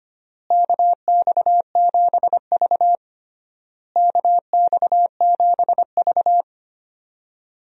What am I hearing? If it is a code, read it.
Morse code, heard as a keyed tone, about 25 wpm, 700 Hz. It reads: KX7V KX7V
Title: USA Call Signs - 4 Characters - Repeated 3x - 25wpm